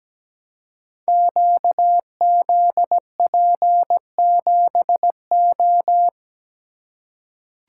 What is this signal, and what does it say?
Morse code, heard as a keyed tone, about 17 wpm, 700 Hz. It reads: QZP7O
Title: Code Groups - 5 Letter:Numbers - 17wpm